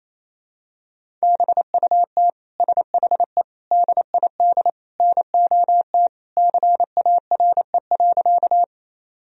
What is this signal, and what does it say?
Morse code, heard as a keyed tone, about 28 wpm, 700 Hz. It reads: BUT SHE DID NOT CARE.
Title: Sentences from Top 1000 Words - Review Part 1 - 28wpm